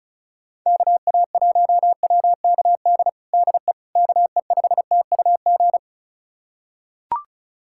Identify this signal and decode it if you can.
Morse code, heard as a keyed tone, about 36 wpm, 690 Hz. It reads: KA1WKD DE KE5TUG
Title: USA Callsign DE Callsign - 6 Characters - 35wpm